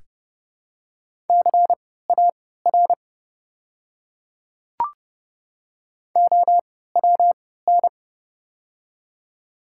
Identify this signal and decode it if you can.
Morse code, heard as a keyed tone, about 30 wpm, spaced as for 17 wpm, 700 Hz. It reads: CAR OWN
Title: 3x Character Spacing - Top 500 Words - 30wpm